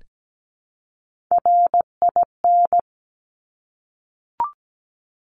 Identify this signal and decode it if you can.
Morse code, heard as a keyed tone, about 17 wpm, 700 Hz. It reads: RIN
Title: Top Three Letter Combinations in Words 17wpm